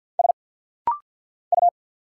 Morse code, heard as a keyed tone, about 47 wpm, 690 Hz. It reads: S U
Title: Single Letter-Number Alt - Rapid-Fire 50wpm